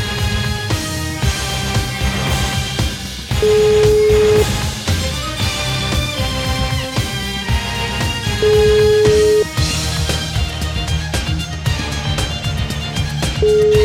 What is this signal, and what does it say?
Ringback tone (Europe, 425 Hz), with noise.